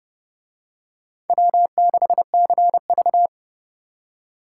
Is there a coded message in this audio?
Yes: Morse code from 1.3 to 3.3 s, heard as a keyed tone, about 29 wpm, 700 Hz. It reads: W6CV